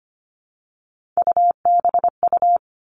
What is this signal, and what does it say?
Morse code, heard as a keyed tone, about 25 wpm, 700 Hz. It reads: UBU
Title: Three Challenging Letter-Number Combinations 25wpm